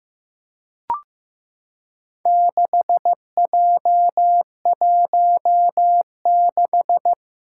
Morse code, heard as a keyed tone, about 15 wpm, 700 Hz. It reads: 6J16
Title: Four Challenging Letter-Number Combinations 15wpm